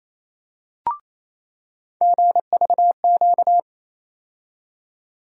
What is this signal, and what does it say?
Morse code, heard as a keyed tone, about 28 wpm, 700 Hz. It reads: GVQ
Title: USA Call Sign Postfixes - 3 Letters 28wpm